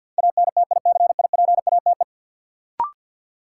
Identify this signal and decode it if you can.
Morse code, heard as a keyed tone, about 53 wpm, 700 Hz. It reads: ANTICIPATE
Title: Sentences from Top 1500 Words - New Word Drill Alt - 50wpm